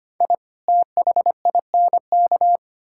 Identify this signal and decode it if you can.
Morse code, heard as a keyed tone, about 25 wpm, 700 Hz. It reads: I THINK
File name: practice / 25wpm / Sentences with 4-Word Phrases - 25wpm